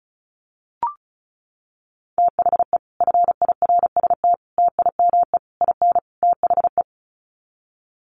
Morse code, heard as a keyed tone, about 34 wpm, 700 Hz. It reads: THE FIRST TIME IN THE
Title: Sets of 5 Words Encore 35wpm